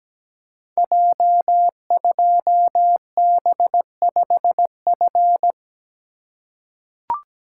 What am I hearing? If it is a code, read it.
Morse code, heard as a keyed tone, about 17 wpm, 700 Hz. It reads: J2B5F
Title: Code Groups - 5 Letter:Numbers - 17wpm